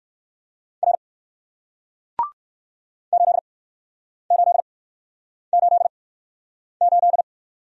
Morse code, heard as a keyed tone, about 57 wpm, spaced as for 9 wpm, 700 Hz. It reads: R 8888